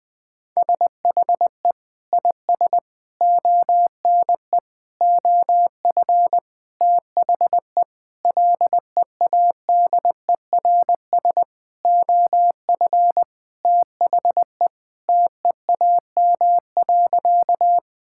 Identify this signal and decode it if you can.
Morse code, heard as a keyed tone, about 20 wpm, 700 Hz. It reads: SHE IS ONE OF THE LEADERS OF THE TEAM.
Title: Sentences with 6-Word Phrases - 20wpm